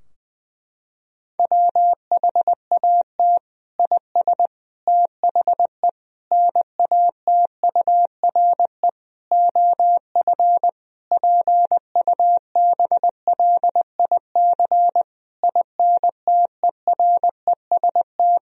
Morse code, heard as a keyed tone, about 20 wpm, 700 Hz. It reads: WHAT IS THE NATURE OF PUBLIC INTEREST
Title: Sentences from Top 1000 Words - Review Part 3 - 20wpm